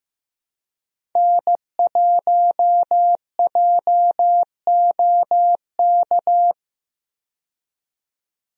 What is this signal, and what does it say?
Morse code, heard as a keyed tone, about 15 wpm, 700 Hz. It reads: N1JOK
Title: USA Call Signs - 5 Characters - 15wpm